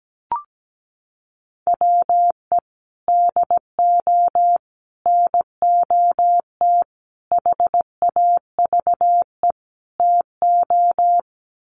Morse code, heard as a keyed tone, about 17 wpm, 700 Hz. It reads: WE DO NOT HAVE TO
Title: Sets of 5 Words 17wpm